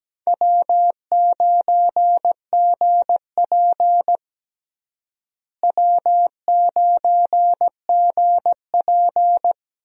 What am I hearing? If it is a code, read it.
Morse code, heard as a keyed tone, about 17 wpm, 700 Hz. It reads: W9GP W9GP
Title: USA Call Signs - 4 Characters - Repeated 2x - 17wpm